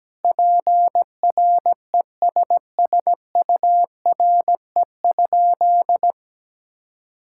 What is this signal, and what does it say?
Morse code, heard as a keyed tone, about 17 wpm, 700 Hz. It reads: PRESSURE?
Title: Sentences from Top 1000 Words - Review Part 6 - 17wpm